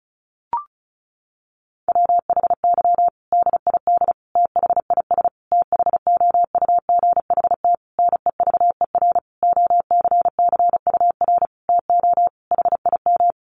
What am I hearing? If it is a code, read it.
Morse code, heard as a keyed tone, about 34 wpm, 700 Hz. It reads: WHY DID THIS THOUGHT NEVER OCCUR TO HIM
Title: Sentences from Top 900 Words 35wpm